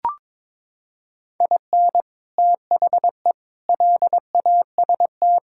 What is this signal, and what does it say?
Morse code, heard as a keyed tone, about 22 wpm, 700 Hz. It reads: IN THE LAST
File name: practice / 22wpm / Sentences with 3-Word Phrases - 22wpm